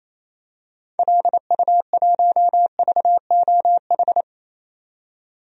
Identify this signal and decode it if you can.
Morse code, heard as a keyed tone, about 28 wpm, 700 Hz. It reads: LU1VOH
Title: DXCC-100 Call Signs - 6 Characters - 28wpm